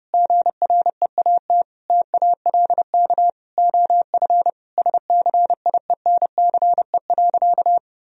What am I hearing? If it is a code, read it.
Morse code, heard as a keyed tone, about 30 wpm, 700 Hz. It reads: GREAT TALK OF SCIENCE.
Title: Sentences from Top 1000 Words - Review Part 2 - 30wpm